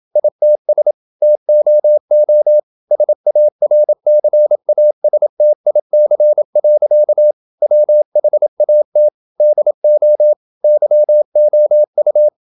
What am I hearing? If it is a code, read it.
Morse code, heard as a keyed tone, about 28 wpm, 600 Hz. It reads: ITS TOO SARCASTIC. WHAT DO YOU
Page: Never 26wpm | CW med Gnister